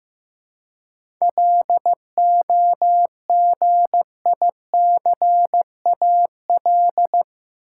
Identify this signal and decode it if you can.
Morse code, heard as a keyed tone, about 15 wpm, 700 Hz. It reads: LOGICAL
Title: Seven Letter Words - 15wpm